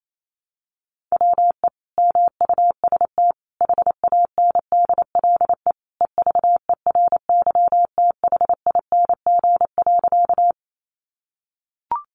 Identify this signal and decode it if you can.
Morse code, heard as a keyed tone, about 28 wpm, 700 Hz. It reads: WE MUST HANDLE EVERYTHING.
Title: Sentences from Top 1100 Words 28wpm